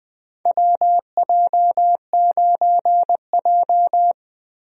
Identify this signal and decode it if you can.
Morse code, heard as a keyed tone, about 21 wpm, 700 Hz. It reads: WJ9J
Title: USA Call Signs - 4 Characters - 20wpm